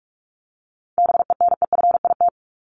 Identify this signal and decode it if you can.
Morse code, heard as a keyed tone, about 44 wpm, 700 Hz. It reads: BENEFIT
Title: Top 751-1000 Words 45wpm